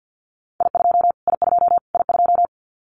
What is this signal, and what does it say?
Morse code, heard as a keyed tone, about 49 wpm, 700 Hz. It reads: I2 I2 I2
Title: Callsign Prefixes for Top 25 DXCC - Repeated 3x - 50wpm